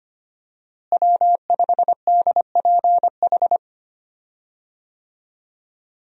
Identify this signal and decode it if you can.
Morse code, heard as a keyed tone, about 25 wpm, 700 Hz. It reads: W5DPH
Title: USA Call Signs - 5 Characters - Repeated 2x - 25wpm